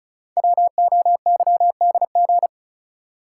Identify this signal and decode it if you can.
Morse code, heard as a keyed tone, about 33 wpm, 690 Hz. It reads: WOYDG
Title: Code Groups - 5 Letters - 35wpm